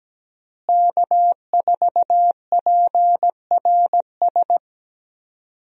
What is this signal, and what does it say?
Morse code, heard as a keyed tone, about 17 wpm, 700 Hz. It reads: K4PRS
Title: USA Call Signs - 5 Characters - Repeated 3x - 17wpm